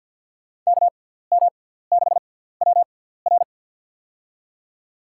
Morse code, heard as a keyed tone, about 52 wpm, spaced as for 17 wpm, 700 Hz. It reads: KM6WR
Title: USA Call Signs - 5 Characters - 6x Character Spacing - 50wpm